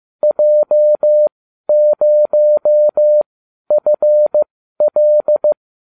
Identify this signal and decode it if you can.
Morse code, heard as a keyed tone, about 15 wpm, spaced as for 12 wpm, 600 Hz. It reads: J0FL